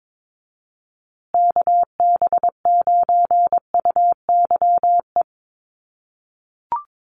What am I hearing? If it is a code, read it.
Morse code, heard as a keyed tone, about 22 wpm, 700 Hz. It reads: KB9UYE
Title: USA Call Signs Repeated 3x 22wpm